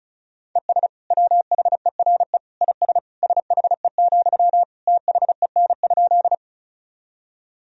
Morse code, heard as a keyed tone, about 34 wpm, 700 Hz. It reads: ES WHERE IS SHE, THEN?
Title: Sentences from Top 200 Words and Abbreviations 35wpm